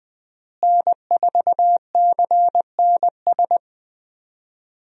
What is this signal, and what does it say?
Morse code, heard as a keyed tone, about 21 wpm, 700 Hz. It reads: N4CNS
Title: USA Call Signs - 5 Characters - Repeated 2x - 20wpm